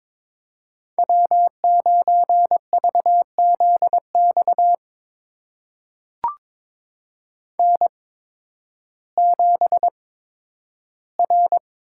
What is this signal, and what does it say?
Morse code, heard as a keyed tone, about 22 wpm, spaced as for 12 wpm, 700 Hz. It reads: W9VZX N7R
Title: USA Call Signs - 5 Characters - 8x Character Spacing - 22wpm